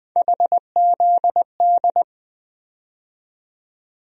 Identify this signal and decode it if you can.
Morse code, heard as a keyed tone, about 20 wpm, 700 Hz. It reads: HZD
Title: USA Call Sign Postfixes - 3 Letters 20wpm